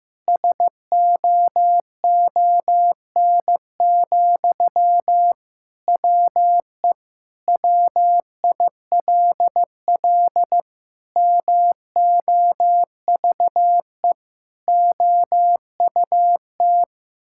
Morse code, heard as a keyed tone, about 15 wpm, 700 Hz. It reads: SOON, WE WILL MOVE OUT
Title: Sentences from Top 1000 Words - Review Part 6 - 15wpm